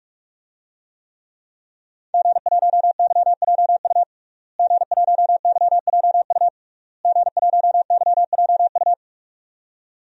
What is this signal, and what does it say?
Morse code, heard as a keyed tone, about 46 wpm, 700 Hz. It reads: G1YJU G1YJU G1YJU
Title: DXCC-100 Call Signs - 5 Characters - Repeated 3x - 45wpm